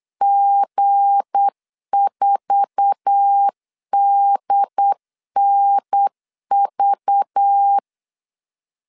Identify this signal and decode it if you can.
Morse code, heard as a keyed tone, about 8 wpm, 800 Hz. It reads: G4DNV
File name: LEZIONI / 8 YJ45 / 8 YJ4 8wpm 3246